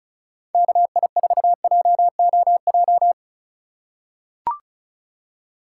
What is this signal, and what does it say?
Morse code, heard as a keyed tone, about 34 wpm, 690 Hz. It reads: KI4JOJ
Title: USA Call Signs - 6 Characters - 2x Character Spacing - 35wpm